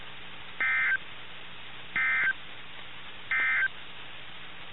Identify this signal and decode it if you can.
Emergency Alert System (SAME) tones, heard as harsh buzzing bursts. It is heard from 0.5 to 3.8 s.
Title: NOAA Weather Radio - KIH61